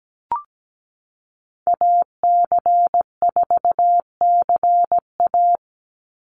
Morse code, heard as a keyed tone, about 17 wpm, 700 Hz. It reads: AC4CA
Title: Call Signs Repeated 3x - 17wpm